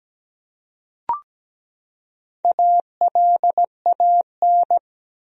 Morse code, heard as a keyed tone, about 17 wpm, 700 Hz. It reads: ALAN
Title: USA Common Names - Max 4 Letters - 17wpm